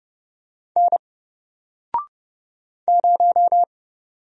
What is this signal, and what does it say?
Morse code, heard as a keyed tone, about 29 wpm, 700 Hz. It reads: N 0
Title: Single Letter-Number Alt 30wpm